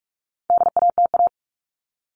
Morse code, heard as a keyed tone, about 44 wpm, 700 Hz. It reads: DATA
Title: Top 501-750 Words 45wpm